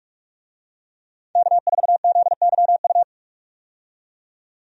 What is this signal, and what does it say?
Morse code, heard as a keyed tone, about 41 wpm, 700 Hz. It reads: K4ZYU
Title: USA Call Signs - 5 Characters - 45wpm